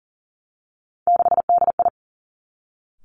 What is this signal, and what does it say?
Morse code, heard as a keyed tone, about 42 wpm, 700 Hz. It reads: 6DI
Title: Three Challenging Letter-Number Combinations 40wpm